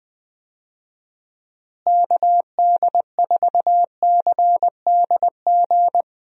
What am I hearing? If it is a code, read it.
Morse code, heard as a keyed tone, about 21 wpm, 700 Hz. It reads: KD4CDG
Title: USA Call Signs - 6 Characters - Repeated 2x - 20wpm